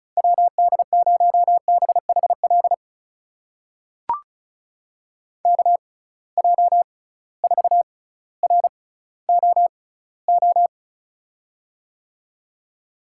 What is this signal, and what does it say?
Morse code, heard as a keyed tone, about 36 wpm, spaced as for 18 wpm, 700 Hz. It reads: WD0BHL KJ4ROO
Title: USA Call Signs - 6 Characters - 6x Character Spacing - 35wpm